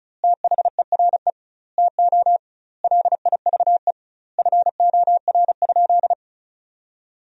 Morse code, heard as a keyed tone, about 36 wpm, 700 Hz. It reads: THERE TO LIVE FOR?
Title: 2x Word-Spacing - Sentences from Top 200 Words - 35wpm